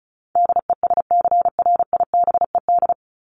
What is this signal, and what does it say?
Morse code, heard as a keyed tone, about 34 wpm, 700 Hz. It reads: DESCRIBED
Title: ED Suffix - Top 100 Words with Base Words 35wpm